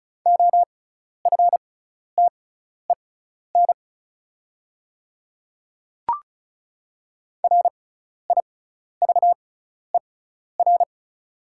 Morse code, heard as a keyed tone, about 36 wpm, spaced as for 12 wpm, 690 Hz. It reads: OFTEN RIVER
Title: 6x Character Spacing - Top 300 Words - 35wpm